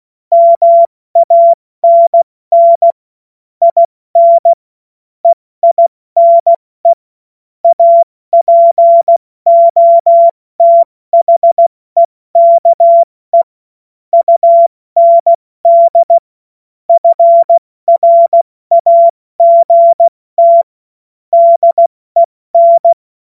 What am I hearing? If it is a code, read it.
Morse code, heard as a keyed tone, about 16 wpm, 680 Hz. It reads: MANN IN EINE APOTHEKE UND FRAGT DEN
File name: Morsekurs / Witze / Witz 02 Januar 2024 70 F80